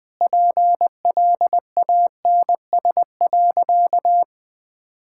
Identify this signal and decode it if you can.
Morse code, heard as a keyed tone, about 20 wpm, 700 Hz. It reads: PLANS.
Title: Sentences with 5-Word Phrases - 20wpm